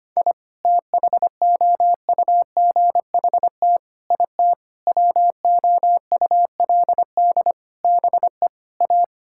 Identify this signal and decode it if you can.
Morse code, heard as a keyed tone, about 25 wpm, 700 Hz. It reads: I THOUGHT IT WOULD BE A
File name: practice / 25wpm / Sentences with 9-Word Phrases - 25wpm